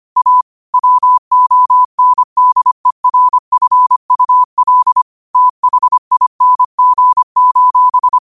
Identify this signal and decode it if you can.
Morse code, heard as a keyed tone, about 25 wpm, 1000 Hz. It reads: A WONDERFUL THING:
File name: qotdfiles / 25wpm